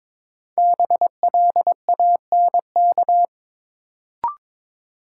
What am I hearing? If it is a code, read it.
Morse code, heard as a keyed tone, about 22 wpm, 700 Hz. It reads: BLANK
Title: Sentences from Top 1500 Words - New Word Drill - 22wpm